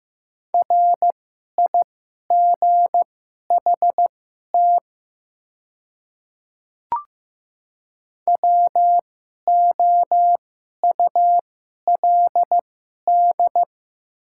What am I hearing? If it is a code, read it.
Morse code, heard as a keyed tone, about 15 wpm, spaced as for 11 wpm, 700 Hz. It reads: RIGHT WOULD
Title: 2x Character Spacing - Top 300 Words - 15wpm